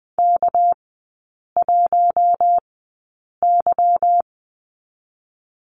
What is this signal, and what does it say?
Morse code, heard as a keyed tone, about 20 wpm, spaced as for 8 wpm, 700 Hz. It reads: K1Y
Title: 198 Lesson 32e - Letter k - Three Characters 20wpm